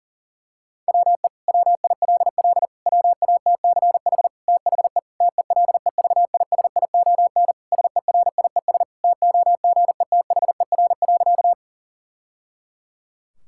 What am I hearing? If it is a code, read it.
Morse code, heard as a keyed tone, about 40 wpm, 700 Hz. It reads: WE WILL WATCH THE TELEVISION SERIES TOGETHER.
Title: Sentences from Top 1000 Words - Review Part 5 - 40wpm